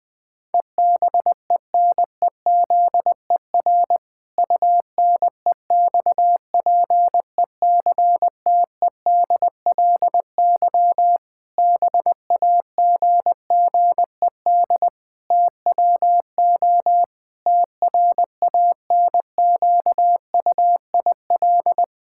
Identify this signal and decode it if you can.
Morse code, heard as a keyed tone, about 19 wpm, 700 Hz. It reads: EBENEZER UNEXPECTEDLY BAGGED TWO TRANQUIL
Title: Pangram Challenge v2 - 20wpm